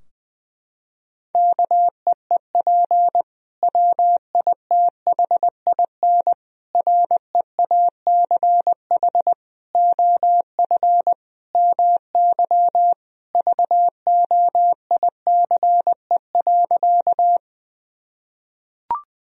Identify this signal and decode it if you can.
Morse code, heard as a keyed tone, about 20 wpm, 700 Hz. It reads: KEEP WITHIN REACH OF MY VOICE.
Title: Sentences from Top 1000 Words - Review Part 4 - 20wpm